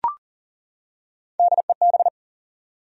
Morse code, heard as a keyed tone, about 40 wpm, 700 Hz. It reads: DEB
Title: USA Common Names - Max 4 Letters - 40wpm